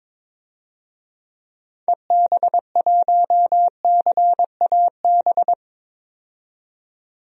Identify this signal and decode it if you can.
Morse code, heard as a keyed tone, about 22 wpm, 700 Hz. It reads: EB1CAB
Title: DXCC-100 Call Signs - 6 Characters - 22wpm